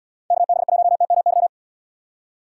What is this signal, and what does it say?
Morse code, heard as a keyed tone, about 74 wpm, 700 Hz. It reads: KB1TMY